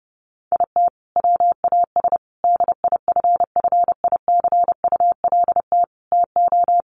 Morse code, heard as a keyed tone, about 30 wpm, 700 Hz. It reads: IT WAS DIFFICULT TO
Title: Sentences from Top 1500 Words 30wpm